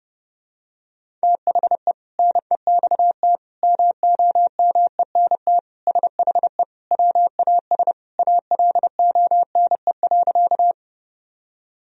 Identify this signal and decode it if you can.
Morse code, heard as a keyed tone, about 30 wpm, 700 Hz. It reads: THE NEXT MOMENT SHE WAS ALONE.
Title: Sentences from Top 1000 Words - Review Part 1 - 30wpm